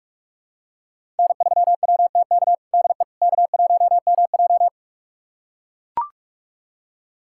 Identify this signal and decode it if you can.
Morse code, heard as a keyed tone, about 41 wpm, 700 Hz. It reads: N3WTK DE K1MJ
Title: USA Callsign DE Callsign 45wpm